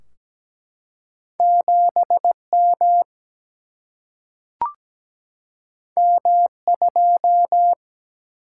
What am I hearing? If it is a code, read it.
Morse code, heard as a keyed tone, about 17 wpm, 700 Hz. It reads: 7 M M 2
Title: Single-Letter Single-Number 17wpm